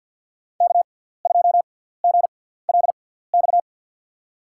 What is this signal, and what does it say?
Morse code, heard as a keyed tone, about 53 wpm, spaced as for 17 wpm, 700 Hz. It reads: K2GLX
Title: USA Call Signs - 6x Character Spacing - 50wpm